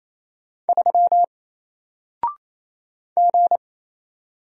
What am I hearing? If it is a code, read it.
Morse code, heard as a keyed tone, about 28 wpm, 700 Hz. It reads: 3 G
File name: Single-Letter-Number-28wpm